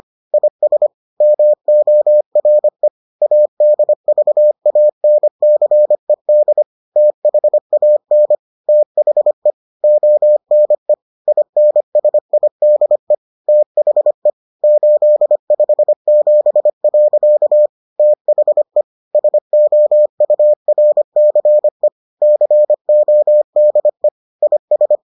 Morse code, heard as a keyed tone, about 25 wpm, 600 Hz. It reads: IS MORE ADVANCED THAN THE ONE INSIDE THE 857. THE SOURCE CODE IS